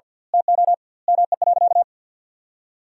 Morse code, heard as a keyed tone, about 50 wpm, 700 Hz. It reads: TO ME.
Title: 2x Word-Spacing - Sentences from Top 400 Words - 50wpm